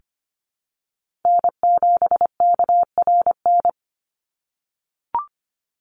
Morse code, heard as a keyed tone, about 25 wpm, 700 Hz. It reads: N7KRN
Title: NA SOTA Chaser Callsigns 25wpm